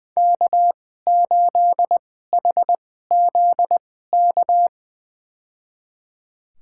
Morse code, heard as a keyed tone, about 20 wpm, spaced as for 15 wpm, 700 Hz. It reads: K8HZK
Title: USA Call Signs - 2x Character Spacing - 20wpm